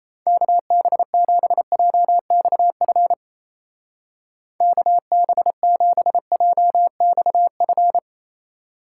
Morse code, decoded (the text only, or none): KB7JXF KB7JXF